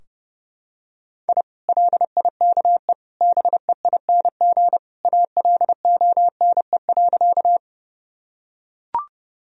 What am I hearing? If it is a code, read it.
Morse code, heard as a keyed tone, about 30 wpm, 700 Hz. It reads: I LIKE BEING ALONE.
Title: Sentences from Top 1000 Words - Review Part 3 - 30wpm